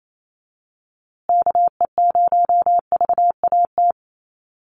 Morse code, heard as a keyed tone, about 28 wpm, 700 Hz. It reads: KE0VAT
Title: USA Call Signs Repeated 2x 28wpm